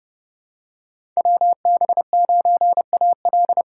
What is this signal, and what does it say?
Morse code, heard as a keyed tone, about 32 wpm, 700 Hz. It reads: WB9AL
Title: USA Call Signs - 5 Characters - 3x Character Spacing - 30wpm